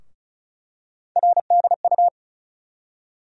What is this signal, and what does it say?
Morse code, heard as a keyed tone, about 36 wpm, 700 Hz. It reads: RDU
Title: USA Call Sign Postfixes - 3 Letters 35wpm